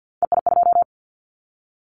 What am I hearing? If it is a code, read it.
Morse code, heard as a keyed tone, about 50 wpm, 700 Hz. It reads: EI2
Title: Callsign Prefixes for Top 50 DXCC - 50wpm